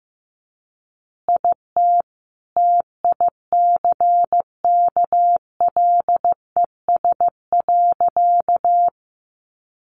Morse code, heard as a keyed tone, about 15 wpm, 700 Hz. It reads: IT TICKLES.
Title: Additional Easy Phrases 15wpm